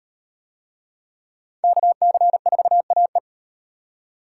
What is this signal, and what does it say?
Morse code, heard as a keyed tone, about 38 wpm, 700 Hz. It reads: KC4AE